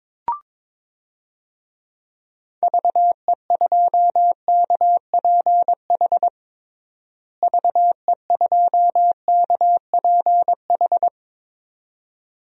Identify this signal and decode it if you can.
Morse code, heard as a keyed tone, about 22 wpm, 700 Hz. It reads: VE2KPH VE2KPH
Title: DXCC-100 Call Signs - 6 Characters - Repeated 2x - 22wpm